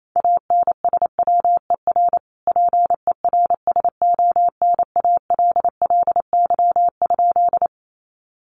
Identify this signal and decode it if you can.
Morse code, heard as a keyed tone, about 28 wpm, 700 Hz. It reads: ANSWER PERSONALLY?
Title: Sentences from Top 1400 Words 28wpm